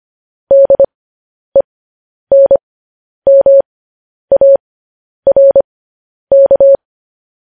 Morse code, heard as a keyed tone, about 25 wpm, spaced as for 10 wpm, 550 Hz. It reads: DENMARK
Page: news-headlines-in-morse-code-at-10-wpm